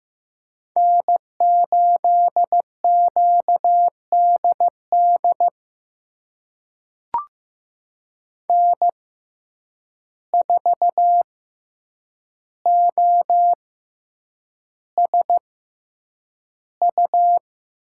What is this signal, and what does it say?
Morse code, heard as a keyed tone, about 15 wpm, spaced as for 8 wpm, 700 Hz. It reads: N8QDD N4OSU
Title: USA Call Signs - 5 Characters - 6x Character Spacing - 15wpm